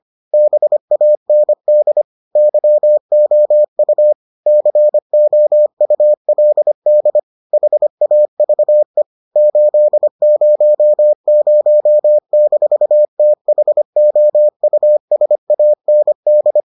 Morse code, heard as a keyed tone, about 25 wpm, 600 Hz. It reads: BAND YOU COULD HAVE 800-THOUSAND